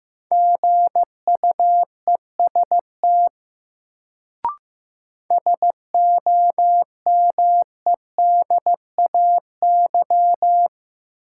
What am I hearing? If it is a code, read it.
Morse code, heard as a keyed tone, about 15 wpm, 700 Hz. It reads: GUEST SOMEDAY
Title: Sentences from Top 1600 Words - New Word Drill - 15wpm